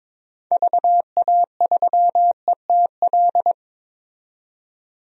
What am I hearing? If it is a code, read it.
Morse code, heard as a keyed tone, about 22 wpm, 700 Hz. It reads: VA3ETL
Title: DXCC-100 Call Signs - 6 Characters - 22wpm